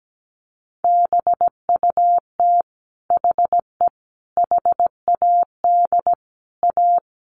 Morse code, heard as a keyed tone, about 17 wpm, 700 Hz. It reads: BUT HE HAD A
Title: Sentences from Top 1200 Words 17wpm